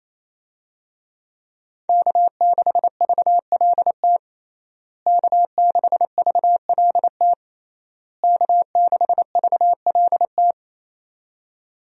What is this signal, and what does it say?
Morse code, heard as a keyed tone, about 28 wpm, 700 Hz. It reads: K6VLT K6VLT K6VLT
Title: USA Call Signs - 5 Characters - Repeated 3x - 28wpm